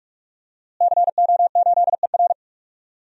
Morse code, heard as a keyed tone, about 46 wpm, 700 Hz. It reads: CO8ER
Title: DXCC-100 Call Signs - 5 Characters - Repeated 2x - 45wpm